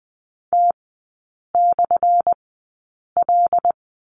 Morse code, heard as a keyed tone, about 19 wpm, spaced as for 8 wpm, 700 Hz. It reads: T/L
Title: 191 Lesson 31e - Letter v - Three Characters 20wpm